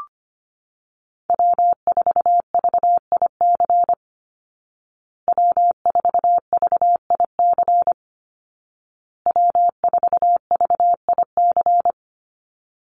Morse code, heard as a keyed tone, about 25 wpm, 700 Hz. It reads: W4VIC W4VIC W4VIC
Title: Call Signs Repeated 2x - 25wpm